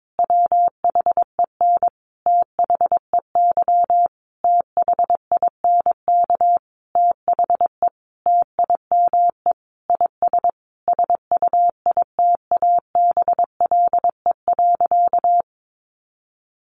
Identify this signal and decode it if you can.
Morse code, heard as a keyed tone, about 22 wpm, 700 Hz. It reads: WHEN THEY THINK THE TIME IS SUITABLE.
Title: Sentences from Top 1400 Words 22wpm